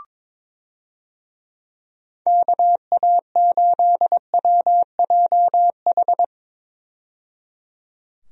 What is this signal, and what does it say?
Morse code, heard as a keyed tone, about 21 wpm, 700 Hz. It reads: KA8WJH
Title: USA Call Signs - 6 Characters - 22wpm